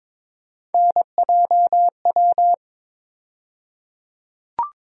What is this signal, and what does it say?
Morse code, heard as a keyed tone, about 22 wpm, 700 Hz. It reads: NJW
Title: USA Call Sign Postfixes - 3 Letters 22wpm